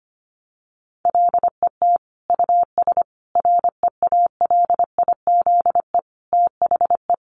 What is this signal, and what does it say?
Morse code, heard as a keyed tone, about 25 wpm, 700 Hz. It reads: LET US REALIZE THE
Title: Sentences from Top 1300 Words 25wpm